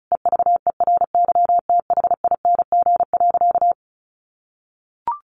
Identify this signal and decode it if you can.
Morse code, heard as a keyed tone, about 34 wpm, 700 Hz. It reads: EVERYTHING.
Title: Sentences from Top 700 Words 35wpm